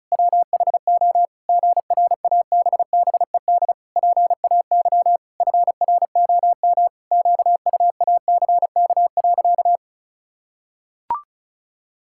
Morse code, heard as a keyed tone, about 34 wpm, 700 Hz. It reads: WHO GRABBED PAY FROM QUACK.
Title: Pangram Challenge v2 - 35wpm